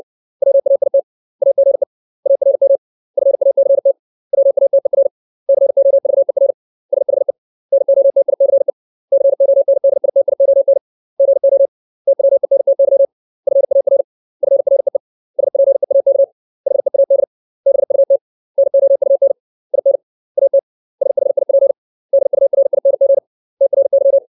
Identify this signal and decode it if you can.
Morse code, heard as a keyed tone, about 60 wpm, 550 Hz. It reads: WAIT AGE ARM FACT WATER COVER SHE NOTICE CONDITION GO TWENTY FAR FREE SPEAK HEAD BUT NOUN IN AT SHIP DURING TRY
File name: 632 words / 632 Words 60WPM